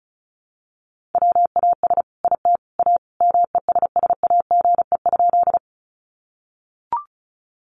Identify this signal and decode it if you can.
Morse code, heard as a keyed tone, about 34 wpm, 700 Hz. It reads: WAS IT A MESSAGE?
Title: Sentences from Top 1100 Words 35wpm